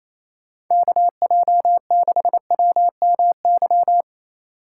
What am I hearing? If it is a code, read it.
Morse code, heard as a keyed tone, about 28 wpm, 700 Hz. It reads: KJ6WMY
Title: USA Call Signs - 6 Characters - 4x Character Spacing - 28wpm